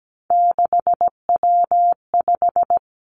Morse code, heard as a keyed tone, about 17 wpm, 700 Hz. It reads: 6W5
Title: Three Challenging Letter-Number Combinations 17wpm